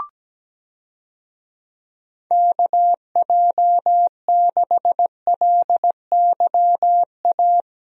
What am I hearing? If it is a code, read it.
Morse code, heard as a keyed tone, about 17 wpm, 700 Hz. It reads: KJ6LYA
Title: USA Call Signs - 6 Characters - Repeated 2x - 17wpm